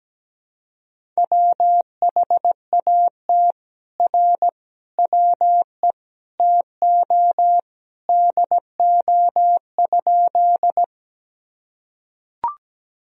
Morse code, heard as a keyed tone, about 17 wpm, 700 Hz. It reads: WHAT R WE TO DO?
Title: Sentences from Top 100 Words and Abbreviations 17wpm